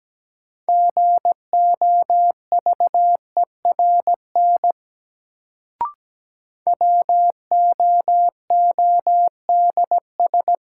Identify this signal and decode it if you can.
Morse code, heard as a keyed tone, about 17 wpm, 700 Hz. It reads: GOVERN WOODS
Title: Sentences from Top 1000 Words - Review Part 2 - New Word Drill - 17wpm